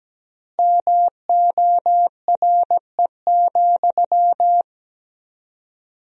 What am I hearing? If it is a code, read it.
Morse code, heard as a keyed tone, about 17 wpm, 700 Hz. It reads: MORE,
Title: 4x Word-Spacing - Sentences from Top 300 Words - 17wpm